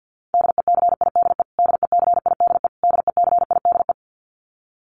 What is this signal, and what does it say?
Morse code, heard as a keyed tone, about 50 wpm, 700 Hz. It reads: DECIDE DECIDE DECIDE
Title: Top 500 Words Repeated 50wpm